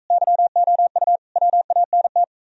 Morse code, heard as a keyed tone, about 43 wpm, 700 Hz. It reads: YOU WANT